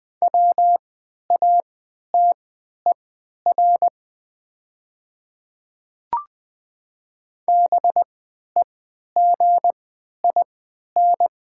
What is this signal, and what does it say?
Morse code, heard as a keyed tone, about 20 wpm, spaced as for 11 wpm, 700 Hz. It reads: WATER BEGIN
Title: 3x Character Spacing - Top 300 Words - 20wpm